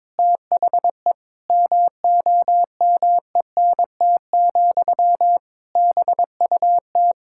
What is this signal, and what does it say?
Morse code, heard as a keyed tone, about 22 wpm, 700 Hz. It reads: THE MOMENT, BUT
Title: Sentences from Top 1000 Words - Review Part 6 - 22wpm